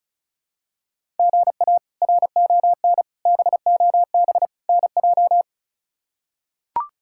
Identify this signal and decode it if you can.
Morse code, heard as a keyed tone, about 36 wpm, 700 Hz. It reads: GA RON BOB NJ
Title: K1USN SST QSO Practice - 35wpm